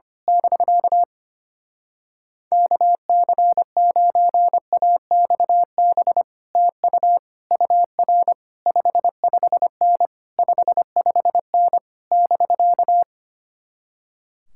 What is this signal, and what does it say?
Morse code, heard as a keyed tone, about 24 wpm, 700 Hz. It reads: <BK> KC9AXB TU UR 55N 55N <BK>
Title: Speed-Racing Common QSO Phrases Encore - 25wpm